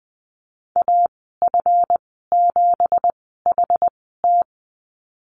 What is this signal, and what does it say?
Morse code, heard as a keyed tone, about 20 wpm, spaced as for 15 wpm, 700 Hz. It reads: AF7HT
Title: 231 Lesson 37g - Letter x - Call Signs 20wpm